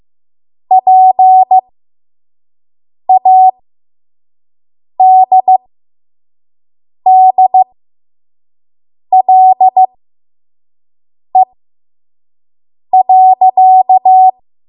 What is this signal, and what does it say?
Morse code, heard as a keyed tone, about 15 wpm, spaced as for 5 wpm, 750 Hz. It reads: PADDLE.
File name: Morse / 05WPM